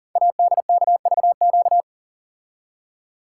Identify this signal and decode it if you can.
Morse code, heard as a keyed tone, about 38 wpm, 690 Hz. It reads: ADKVQ
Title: Code Groups - 5 Letters - 40wpm